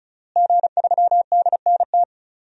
Morse code, heard as a keyed tone, about 36 wpm, 700 Hz. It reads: G3DNT
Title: DXCC-100 Call Signs - 5 Characters - Repeated 3x - 35wpm